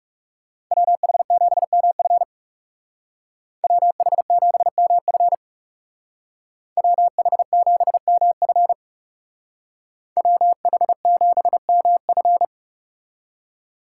Morse code, heard as a keyed tone, about 39 wpm, 700 Hz. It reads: WH7MF WH7MF WH7MF WH7MF